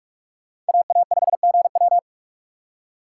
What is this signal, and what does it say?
Morse code, heard as a keyed tone, about 55 wpm, 700 Hz. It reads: AA5GW
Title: Speed Racing - USA Call Signs 45wpm